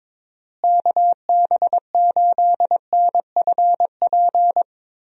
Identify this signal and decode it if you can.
Morse code, heard as a keyed tone, about 22 wpm, 700 Hz. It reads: KB8NFP
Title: USA Call Signs - 6 Characters - Repeated 2x - 22wpm